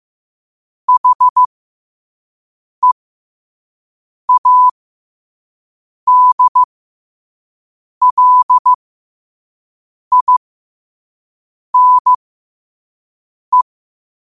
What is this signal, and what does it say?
Morse code, heard as a keyed tone, about 15 wpm, spaced as for 5 wpm, 1000 Hz. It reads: HEADLINE
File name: qotdfiles / 5wpm